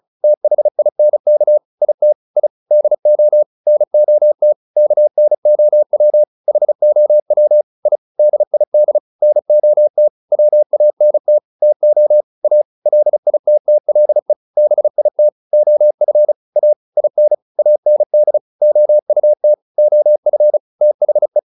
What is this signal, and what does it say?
Morse code, heard as a keyed tone, about 36 wpm, 600 Hz. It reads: THINK IT I DO NOT KNOW HOW I DID NOT WANT TO A LITTLE BIT OF A IN AND OUT OF THE